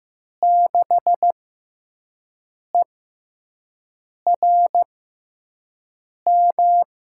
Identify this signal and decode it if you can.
Morse code, heard as a keyed tone, about 15 wpm, spaced as for 5 wpm, 700 Hz. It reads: 6ERM
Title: USA Call Signs - 5 Characters - 6x Character Spacing - 15wpm